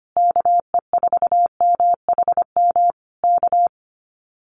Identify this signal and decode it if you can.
Morse code, heard as a keyed tone, about 25 wpm, 700 Hz. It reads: KE4MHM K
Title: Common QSO Phrases Encore 25wpm